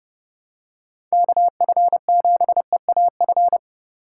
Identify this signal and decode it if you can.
Morse code, heard as a keyed tone, about 30 wpm, 700 Hz. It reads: KF7EAF
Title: USA Call Signs - 4x Character Spacing - 30wpm